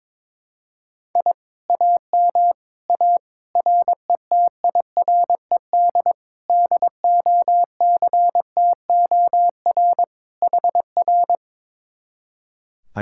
Morse code, heard as a keyed tone, about 22 wpm, 700 Hz. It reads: I AM A RETIRED DOCTOR HR
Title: Speed-Racing Common QSO Phrases Encore - 22wpm